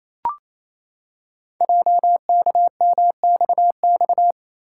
Morse code, heard as a keyed tone, about 28 wpm, 700 Hz. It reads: JKMXX
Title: Code Groups - 5 Letters - 28wpm